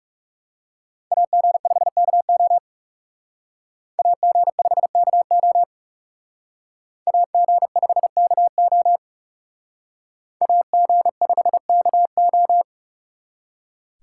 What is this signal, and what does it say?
Morse code, heard as a keyed tone, about 38 wpm, 700 Hz. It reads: AG5KO AG5KO AG5KO AG5KO